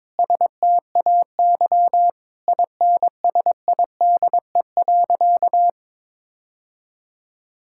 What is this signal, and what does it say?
Morse code, heard as a keyed tone, about 22 wpm, 700 Hz. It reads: STAY INSIDE.
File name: practice / 22wpm / Sentences with 2-Word Phrases - 22wpm